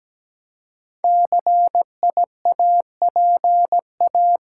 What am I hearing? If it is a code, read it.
Morse code, heard as a keyed tone, about 17 wpm, 700 Hz. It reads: CIAPA
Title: Code Groups - 5 Letters - 17wpm